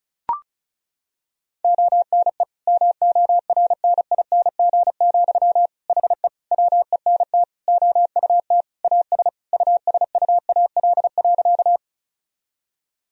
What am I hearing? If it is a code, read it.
Morse code, heard as a keyed tone, about 36 wpm, 700 Hz. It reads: ONE MORNING, HE WENT OUT AS USUAL.
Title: Sentences from Top 1000 Words - Review Part 2 - 35wpm